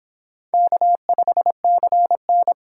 Morse code, heard as a keyed tone, about 25 wpm, 700 Hz. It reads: K5CN